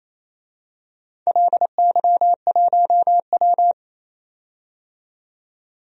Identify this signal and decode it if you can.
Morse code, heard as a keyed tone, about 28 wpm, 700 Hz. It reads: LY1W
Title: DXCC-100 Call Signs - 4 Characters - 28wpm